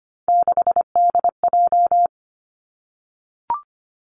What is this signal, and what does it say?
Morse code, heard as a keyed tone, about 25 wpm, 700 Hz. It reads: 6DJ
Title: Three Challenging Letter-Number Combinations 25wpm